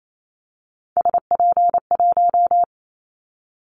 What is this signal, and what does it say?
Morse code, heard as a keyed tone, about 28 wpm, 700 Hz. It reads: SP1
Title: Callsign Prefixes for Top 25 DXCC - 28wpm